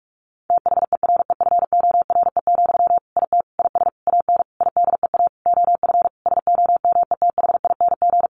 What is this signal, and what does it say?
Morse code, heard as a keyed tone, about 44 wpm, 700 Hz. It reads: THEREFORE, IT IS AN IDEA OF SOMETHING
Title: Sentences from Top 1500 Words 45wpm